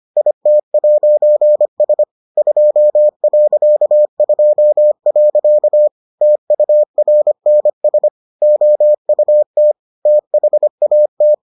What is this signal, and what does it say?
Morse code, heard as a keyed tone, about 25 wpm, 600 Hz. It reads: IT'S 2.2. TURNS OUT THAT